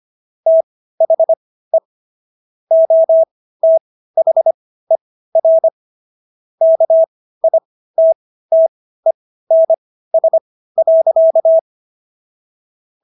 Morse code, heard as a keyed tone, about 25 wpm, spaced as for 15 wpm, 650 Hz. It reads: THE OTHER KITTENS.